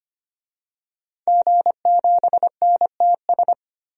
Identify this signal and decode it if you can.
Morse code, heard as a keyed tone, about 25 wpm, 700 Hz. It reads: G7NTS
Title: DXCC-100 Call Signs - 5 Characters - 25wpm